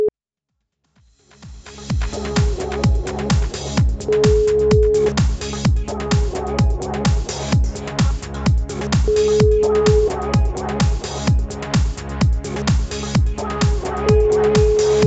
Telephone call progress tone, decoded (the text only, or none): ringback tone